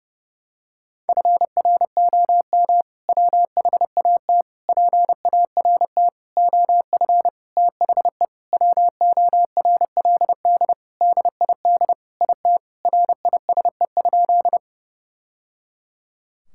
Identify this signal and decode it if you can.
Morse code, heard as a keyed tone, about 30 wpm, 700 Hz. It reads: FROM WHAT PART OF THE WORLD DID IT RISE?
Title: Sentences from Top 1000 Words - Review Part 6 - 30wpm